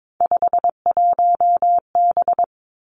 Morse code, heard as a keyed tone, about 22 wpm, 700 Hz. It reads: 51B
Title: Three Challenging Letter-Number Combinations 22wpm